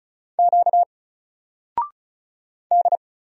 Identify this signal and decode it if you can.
Morse code, heard as a keyed tone, about 35 wpm, 700 Hz. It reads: Q D
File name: Single-Letter-Number-35wpm